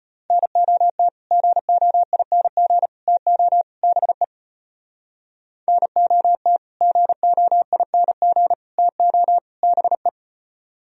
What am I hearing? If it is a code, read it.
Morse code, heard as a keyed tone, about 33 wpm, 700 Hz. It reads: NOT GOING TO BE NOT GOING TO BE